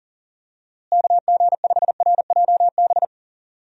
Morse code, heard as a keyed tone, about 38 wpm, 700 Hz. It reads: KG5RJB
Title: USA Call Signs - 6 Characters - Repeated 3x - 40wpm